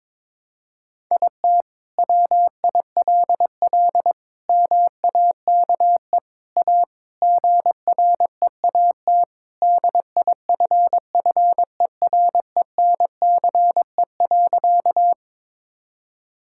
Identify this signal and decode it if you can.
Morse code, heard as a keyed tone, about 22 wpm, 700 Hz. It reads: IT WILL MAKE A GREAT DIFFERENCE.
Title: Sentences from Top 1000 Words - Review Part 5 - 22wpm